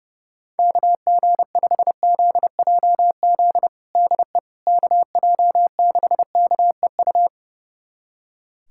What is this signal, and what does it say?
Morse code, heard as a keyed tone, about 29 wpm, 700 Hz. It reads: KG5ZJZ DE KJ6KEU
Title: USA Callsign DE Callsign - 6 Characters - 30wpm